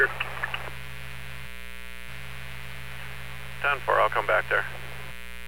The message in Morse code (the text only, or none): none